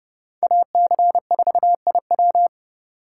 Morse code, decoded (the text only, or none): AC4IW